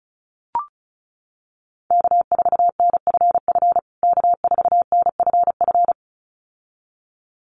Morse code, heard as a keyed tone, about 34 wpm, 700 Hz. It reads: K4NFF K4NFF
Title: USA Call Signs Repeated 2x 35wpm